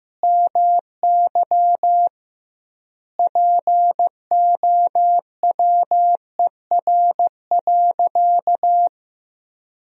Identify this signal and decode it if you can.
Morse code, heard as a keyed tone, about 15 wpm, 700 Hz. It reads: MY POWER.
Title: 2x Word-Spacing - Sentences from Top 500 Words - 15wpm